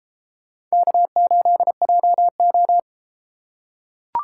Morse code, heard as a keyed tone, about 32 wpm, 700 Hz. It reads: K8JO